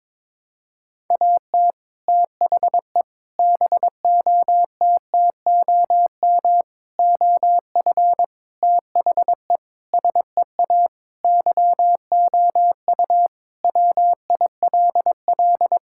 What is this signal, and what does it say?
Morse code, heard as a keyed tone, about 22 wpm, 700 Hz. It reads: AT THE BOTTOM OF THE SEA YOU WILL
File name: practice / 22wpm / Sentences with 3-Word Phrases - 22wpm